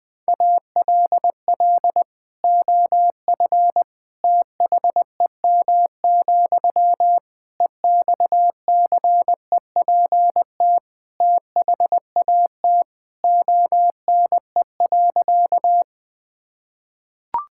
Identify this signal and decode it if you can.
Morse code, heard as a keyed tone, about 20 wpm, 700 Hz. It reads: ALL OF THEM, EXCEPT THAT ONE.
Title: Sentences from Top 1600 Words 20wpm